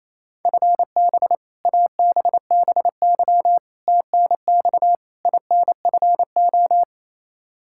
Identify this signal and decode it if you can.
Morse code, heard as a keyed tone, about 28 wpm, 700 Hz. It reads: FB ABBY TNX INFO
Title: Practice QSO - Exchange 3 - More Info - 28wpm